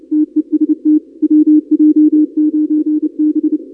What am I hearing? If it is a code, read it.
Morse code, heard as a keyed tone, about 28 wpm, 310 Hz. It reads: TEST WJ9B
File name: contests / 2019-cw-wpx-kt5j-k5pi- -2330 0-31